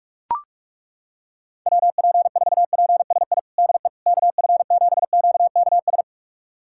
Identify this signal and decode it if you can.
Morse code, heard as a keyed tone, about 42 wpm, 700 Hz. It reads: WP4PSI DE KF7QKS
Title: USA Callsign DE Callsign - 6 Characters - 45wpm